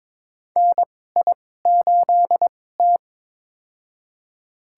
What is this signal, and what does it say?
Morse code, heard as a keyed tone, about 22 wpm, spaced as for 16 wpm, 700 Hz. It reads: NI8T
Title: USA Call Signs - 4 Characters - 2x Character Spacing - 22wpm